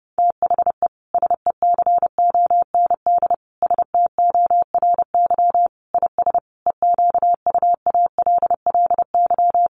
Morse code, heard as a keyed tone, about 28 wpm, 700 Hz. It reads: THE SECOND STORY IS EQUALLY
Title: Sentences from Top 1400 Words 30wpm